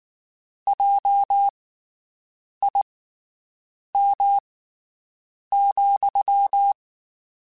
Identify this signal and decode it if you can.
Morse code, heard as a keyed tone, about 19 wpm, spaced as for 7 wpm, 800 Hz. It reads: JIM,
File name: (spaced) / QSOs Spaced-mp3s / WD0BGP 1 @18 sp